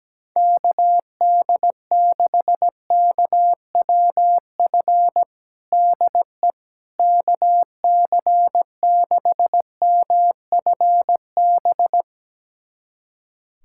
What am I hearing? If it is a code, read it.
Morse code, heard as a keyed tone, about 17 wpm, 700 Hz. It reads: KD6KWF DE KC6MFB
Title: USA Callsign DE Callsign - 6 Characters - 17wpm